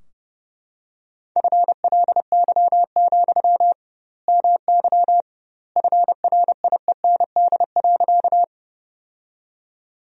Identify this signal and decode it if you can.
Morse code, heard as a keyed tone, about 30 wpm, 700 Hz. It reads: FLY, MY FRIEND.
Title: 2x Word-Spacing - Sentences from Top 500 Words - 30wpm